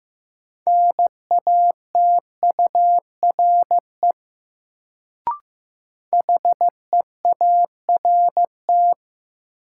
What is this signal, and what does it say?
Morse code, heard as a keyed tone, about 15 wpm, 700 Hz. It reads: NATURE HEART
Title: Sets of 3 Words - New Word Drill - 15wpm